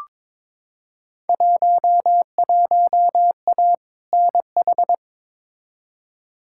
Morse code, heard as a keyed tone, about 22 wpm, 700 Hz. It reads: 11A NH
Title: Field Day Exchanges Encore - 22wpm